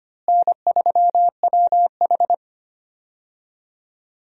Morse code, heard as a keyed tone, about 25 wpm, 700 Hz. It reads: N3WH
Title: USA Call Signs - 4 Characters - Repeated 3x - 25wpm